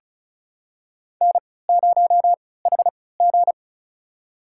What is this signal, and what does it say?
Morse code, heard as a keyed tone, about 36 wpm, spaced as for 20 wpm, 700 Hz. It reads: N0HG
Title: USA Call Signs - 4 Characters - 3x Character Spacing - 35wpm